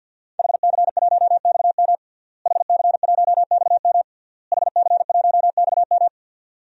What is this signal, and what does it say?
Morse code, heard as a keyed tone, about 50 wpm, 700 Hz. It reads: HC1XM HC1XM HC1XM
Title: DXCC-100 Call Signs - 5 Characters - Repeated 3x - 50wpm